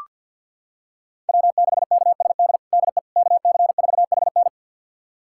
Morse code, heard as a keyed tone, about 46 wpm, 700 Hz. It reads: W6KSD DE KC4HN
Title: USA Callsign DE Callsign - 5 Characters - 50wpm